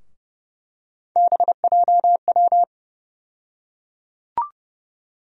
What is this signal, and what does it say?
Morse code, heard as a keyed tone, about 32 wpm, 700 Hz. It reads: BJW
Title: USA Call Sign Postfixes - 3 Letters 30wpm